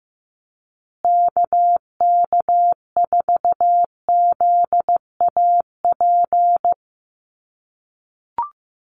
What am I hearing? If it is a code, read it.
Morse code, heard as a keyed tone, about 15 wpm, 700 Hz. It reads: KK4ZAP
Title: USA Call Signs Repeated 2x 15wpm